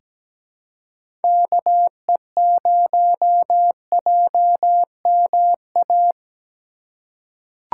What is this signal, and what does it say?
Morse code, heard as a keyed tone, about 17 wpm, 700 Hz. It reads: KE0JMA
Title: USA Call Signs - 6 Characters - 2x Character Spacing - 17wpm